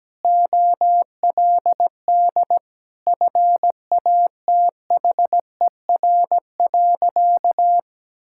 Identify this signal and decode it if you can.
Morse code, heard as a keyed tone, about 17 wpm, 700 Hz. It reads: OLD FATHER.
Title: Sentences from Top 1000 Words - Review Part 5 - 17wpm